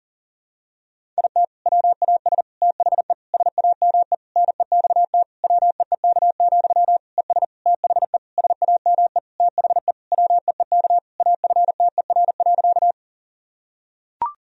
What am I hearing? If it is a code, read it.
Morse code, heard as a keyed tone, about 40 wpm, 700 Hz. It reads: IT WAS THE SAME NEXT WEEK, ES THE SAME THE WEEK AFTER.
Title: Sentences from Top 500 Words and Abbreviations 40wpm